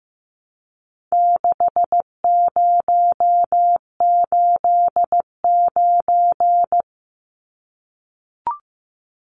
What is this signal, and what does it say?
Morse code, heard as a keyed tone, about 15 wpm, 700 Hz. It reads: 6089
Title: Numbers - 4-Digit 15wpm